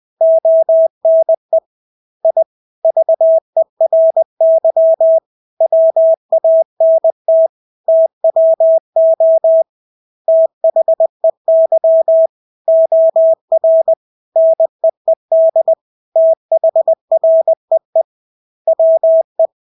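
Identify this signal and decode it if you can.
Morse code, heard as a keyed tone, about 20 wpm, 650 Hz. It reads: ONE I VERY WANT TWO THEY OR NEED THREE WE
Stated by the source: Morse practice of English words at 20 wpm.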